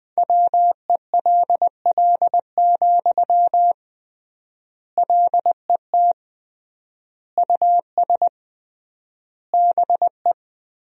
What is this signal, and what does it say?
Morse code, heard as a keyed tone, about 20 wpm, 700 Hz. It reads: WELL, LET US BE
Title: 3x Word-Spacing - Sentences from Top 300 Words - 20wpm